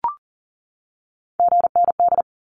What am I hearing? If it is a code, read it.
Morse code, heard as a keyed tone, about 40 wpm, 700 Hz. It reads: GND
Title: Abbreviations 40wpm